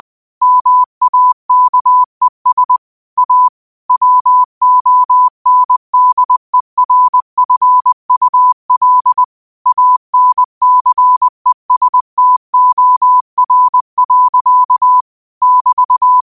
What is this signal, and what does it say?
Morse code, heard as a keyed tone, about 20 wpm, 1000 Hz. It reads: MAKES A WONDERFUL ANCESTOR. <BT>
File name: qotdfiles / 20wpm